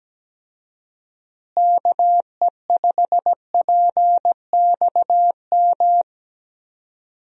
Morse code, heard as a keyed tone, about 17 wpm, 700 Hz. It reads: KE5PXM
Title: USA Call Signs - 6 Characters - Repeated 3x - 17wpm